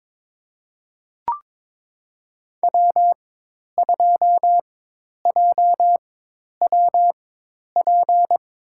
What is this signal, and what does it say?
Morse code, heard as a keyed tone, about 22 wpm, spaced as for 10 wpm, 700 Hz. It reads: W2JWP
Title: USA Call Signs - 4x Character Spacing - 22wpm